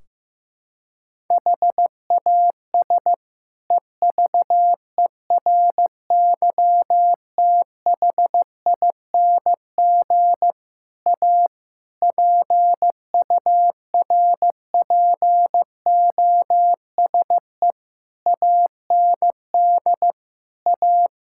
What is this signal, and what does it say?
Morse code, heard as a keyed tone, about 15 wpm, 700 Hz. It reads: HAS EVERYTHING A PURPOSE AND A
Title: Sentences from Top 1600 Words 15wpm